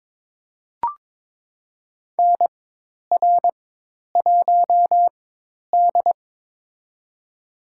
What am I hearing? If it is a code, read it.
Morse code, heard as a keyed tone, about 22 wpm, spaced as for 10 wpm, 700 Hz. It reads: NR1D
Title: USA Call Signs - 4 Characters - 4x Character Spacing - 22wpm